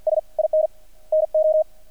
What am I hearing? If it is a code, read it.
Morse code, heard as a keyed tone, about 50 wpm, 620 Hz. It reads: S IN NO
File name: archived / CRSnet